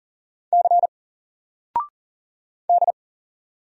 Morse code, heard as a keyed tone, about 40 wpm, 700 Hz. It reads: C D
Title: Single Character (Letter:Number:Punctuation:Prosign) 40wpm